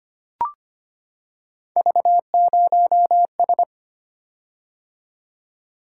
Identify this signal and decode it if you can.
Morse code, heard as a keyed tone, about 25 wpm, 700 Hz. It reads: V0S
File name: Three Letter-Number Combinations - 25wpm